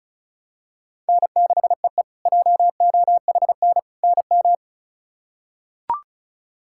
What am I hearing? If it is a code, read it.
Morse code, heard as a keyed tone, about 36 wpm, 700 Hz. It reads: N6EE JOHN NM
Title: K1USN SST QSO Practice - 35wpm